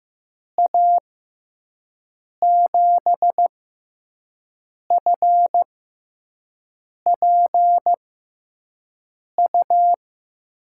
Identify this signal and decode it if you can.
Morse code, heard as a keyed tone, about 15 wpm, spaced as for 5 wpm, 700 Hz. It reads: A7FPU
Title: USA Call Signs - 6 Characters - 6x Character Spacing - 15wpm